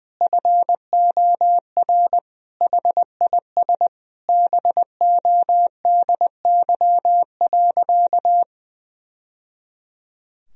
Morse code, heard as a keyed tone, about 20 wpm, 700 Hz. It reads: FOR HIS BODY.
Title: Sentences from Top 1000 Words - Review Part 2 - 20wpm